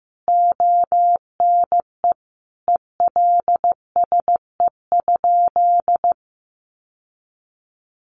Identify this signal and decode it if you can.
Morse code, heard as a keyed tone, about 15 wpm, 700 Hz. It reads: ONE ELSE?
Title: Sentences from Top 800 Words 15wpm